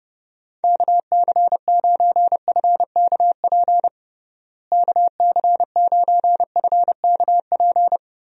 Morse code, heard as a keyed tone, about 31 wpm, 700 Hz. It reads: KC9FKP KC9FKP
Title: USA Call Signs - 6 Characters - Repeated 2x - 30wpm